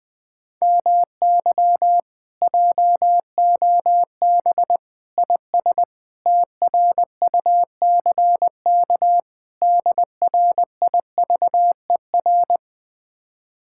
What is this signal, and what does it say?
Morse code, heard as a keyed tone, about 20 wpm, 700 Hz. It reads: MY JOB IS TRUCK DRIVER